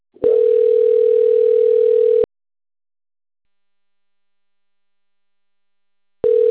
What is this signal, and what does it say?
Ringback tone (North America), 440 + 480 Hz.
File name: rg-1600-unknown- - - .17739